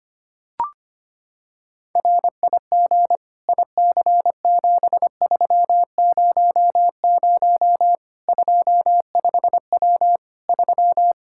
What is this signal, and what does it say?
Morse code, heard as a keyed tone, about 25 wpm, 700 Hz. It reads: RIG IC7300 25W 3
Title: Practice QSO - Exchange 2 - Info - 25wpm